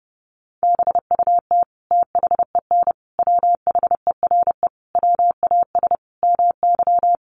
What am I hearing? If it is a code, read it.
Morse code, heard as a keyed tone, about 30 wpm, 700 Hz. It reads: BUT THEN WHERE WAS MY
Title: Sentences from Top 1300 Words 30wpm